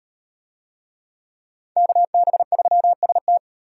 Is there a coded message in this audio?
Yes: Morse code from 1.8 to 3.4 s, heard as a keyed tone, about 37 wpm, 700 Hz. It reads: KB3ST